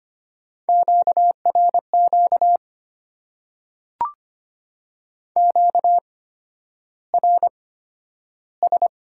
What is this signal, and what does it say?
Morse code, heard as a keyed tone, about 25 wpm, spaced as for 11 wpm, 700 Hz. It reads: QRQ QRS
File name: Top 90 QSO Elements - 8x Character Spacing - 25wpm